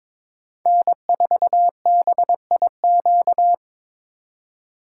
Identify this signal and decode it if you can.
Morse code, heard as a keyed tone, about 21 wpm, 700 Hz. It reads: N4BIQ
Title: USA Call Signs - 5 Characters - 3x Character Spacing - 22wpm